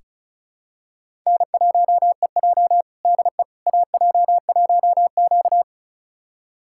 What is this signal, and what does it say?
Morse code, heard as a keyed tone, about 36 wpm, 700 Hz. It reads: N1EJ DE AJ1Q
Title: USA Callsign DE Callsign - 4 Characters - 35wpm